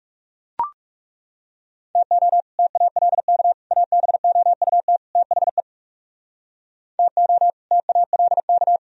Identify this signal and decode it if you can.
Morse code, heard as a keyed tone, about 37 wpm, 700 Hz. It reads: TO TALK ABOUT THE TO TALK
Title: Speed-Racing - Sets of 4 Words 30wpm